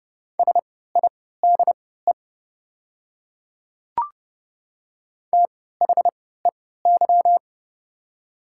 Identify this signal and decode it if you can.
Morse code, heard as a keyed tone, about 31 wpm, spaced as for 17 wpm, 700 Hz. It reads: SIDE THEY
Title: 3x Character Spacing - Top 100 Words - 30wpm